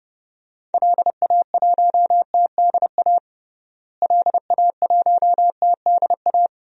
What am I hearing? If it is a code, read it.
Morse code, heard as a keyed tone, about 32 wpm, 700 Hz. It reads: LA1TDA LA1TDA
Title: DXCC-100 Call Signs - 6 Characters - Repeated 2x - 30wpm